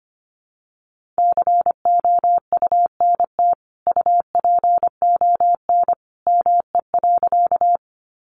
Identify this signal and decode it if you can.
Morse code, heard as a keyed tone, about 25 wpm, 700 Hz. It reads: COUNT UPON ME.
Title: Sentences from Top 1500 Words 25wpm